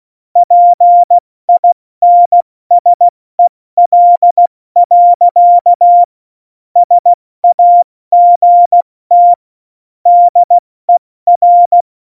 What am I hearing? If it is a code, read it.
Morse code, heard as a keyed tone, about 16 wpm, spaced as for 14 wpm, 700 Hz. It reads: PINSEL. SAGT DER
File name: Morsekurs / Witze / Witz 01. September 2020, 70 F80